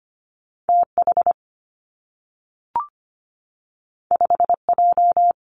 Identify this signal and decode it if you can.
Morse code, heard as a keyed tone, about 25 wpm, 700 Hz. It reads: TH 5J
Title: Two Challenging Letter-Number Combinations 25wpm